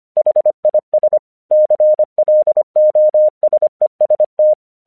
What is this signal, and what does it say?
Morse code, heard as a keyed tone, about 25 wpm, 610 Hz. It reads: HIS CLOSEST
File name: 1 English 25wpm / AQC12